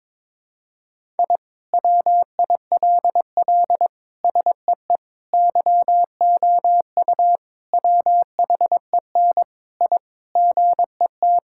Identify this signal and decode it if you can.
Morse code, heard as a keyed tone, about 21 wpm, 700 Hz. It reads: I WILL SEE YOU WHEN I GET
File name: practice / 22wpm / Sentences with 2-Word Phrases - 22wpm